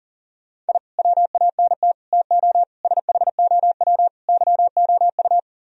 Morse code, heard as a keyed tone, about 51 wpm, 700 Hz. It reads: I WANT TO SHOW YOU